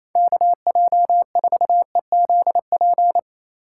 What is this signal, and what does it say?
Morse code, heard as a keyed tone, about 28 wpm, 700 Hz. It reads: KJ4EZP
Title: USA Call Signs - 6 Characters - 3x Character Spacing - 28wpm